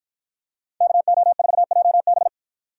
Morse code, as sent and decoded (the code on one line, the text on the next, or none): -.- --- ....- .--- -...
KO4JB